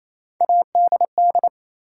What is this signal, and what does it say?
Morse code, heard as a keyed tone, about 28 wpm, 700 Hz. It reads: ADD
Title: Sentences from Top 1000 Words - Review Part 1 - New Word Drill - 28wpm